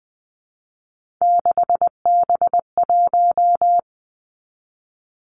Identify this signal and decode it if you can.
Morse code, heard as a keyed tone, about 20 wpm, 700 Hz. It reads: 6B1
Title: Three Challenging Letter-Number Combinations 20wpm